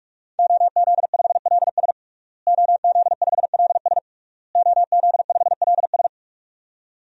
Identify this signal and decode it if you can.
Morse code, heard as a keyed tone, about 46 wpm, 700 Hz. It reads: OZ5LS OZ5LS OZ5LS
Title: DXCC-100 Call Signs - 5 Characters - Repeated 3x - 45wpm